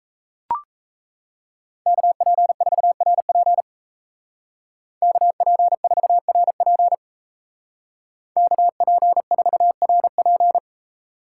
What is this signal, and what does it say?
Morse code, heard as a keyed tone, about 34 wpm, 700 Hz. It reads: KP4RP KP4RP KP4RP
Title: Speed Racing - USA 2x2 Call Signs 28wpm